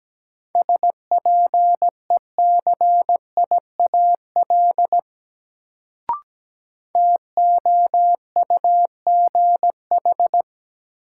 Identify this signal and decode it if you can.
Morse code, heard as a keyed tone, about 17 wpm, 700 Hz. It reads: SPECIAL TOUGH
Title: Sentences from Top 1200 Words - New Word Drill Alt - 17wpm